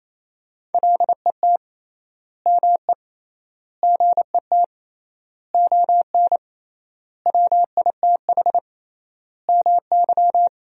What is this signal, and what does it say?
Morse code, heard as a keyed tone, about 28 wpm, 700 Hz. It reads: LET ME GET ON WITH MY
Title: 3x Word-Spacing - Sentences from Top 300 Words - 28wpm